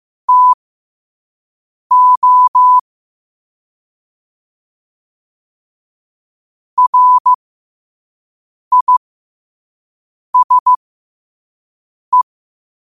Morse code, heard as a keyed tone, about 15 wpm, spaced as for 5 wpm, 1000 Hz. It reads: TO RISE
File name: qotdfiles / 5wpm